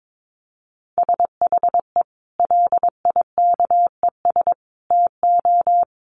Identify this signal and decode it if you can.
Morse code, heard as a keyed tone, about 22 wpm, 700 Hz. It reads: SHE LIKES TO
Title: Easy Phrases 22wpm